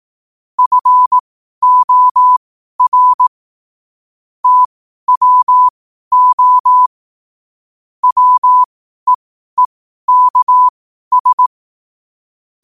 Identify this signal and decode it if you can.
Morse code, heard as a keyed tone, about 18 wpm, spaced as for 12 wpm, 1000 Hz. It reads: FOR TWO WEEKS
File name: 12wpm